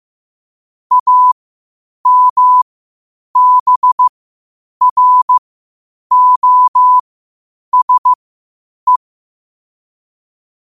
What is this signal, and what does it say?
Morse code, heard as a keyed tone, about 15 wpm, spaced as for 8 wpm, 1000 Hz. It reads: AMBROSE
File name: qotdfiles / 8wpm